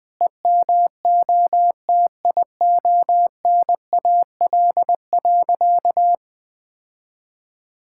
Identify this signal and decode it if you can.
Morse code, heard as a keyed tone, about 20 wpm, 700 Hz. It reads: EMOTIONAL.
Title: Sentences from Top 1000 Words - Review Part 5 - 20wpm